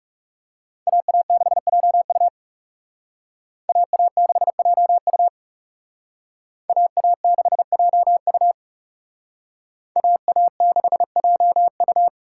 Morse code, heard as a keyed tone, about 36 wpm, 700 Hz. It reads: AA6JU AA6JU AA6JU AA6JU
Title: Speed Racing - USA Call Signs 30wpm